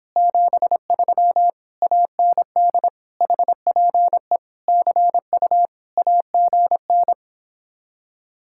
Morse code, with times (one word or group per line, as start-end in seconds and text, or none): 0.2-1.5 73
1.8-2.9 AND
3.2-4.4 HPE
4.7-5.7 CU
6.0-7.1 AGN